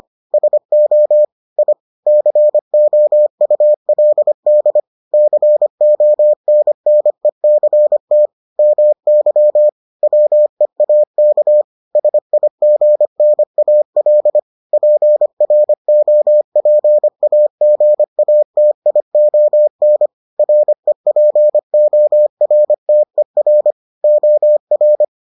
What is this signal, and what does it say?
Morse code, heard as a keyed tone, about 25 wpm, 600 Hz. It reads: SO I COULD CONNECT MY WEAK SIGNAL PROPAGATION REPORTER OR